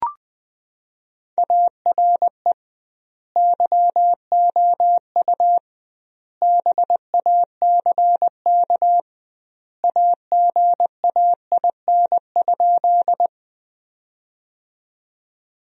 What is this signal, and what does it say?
Morse code, heard as a keyed tone, about 20 wpm, 700 Hz. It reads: ARE YOU BACK AGAIN?
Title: 2x Word-Spacing - Sentences from Top 200 Words - 20wpm